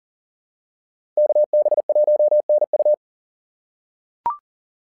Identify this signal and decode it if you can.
Morse code, heard as a keyed tone, about 40 wpm, 600 Hz. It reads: KB1NU
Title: SOTA QSO Practice 40wpm